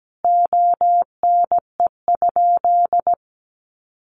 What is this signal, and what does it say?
Morse code, heard as a keyed tone, about 17 wpm, 700 Hz. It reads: ONE?
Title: Sentences from Top 900 Words 17wpm